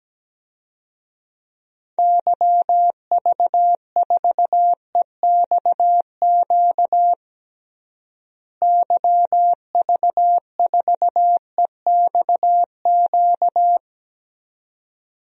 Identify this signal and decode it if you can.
Morse code, heard as a keyed tone, about 17 wpm, 700 Hz. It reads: YV4EXQ YV4EXQ
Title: DXCC-100 Call Signs - 6 Characters - Repeated 2x - 17wpm